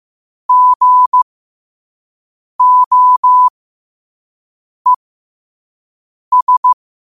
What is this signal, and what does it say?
Morse code, heard as a keyed tone, about 15 wpm, spaced as for 5 wpm, 1000 Hz. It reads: GOES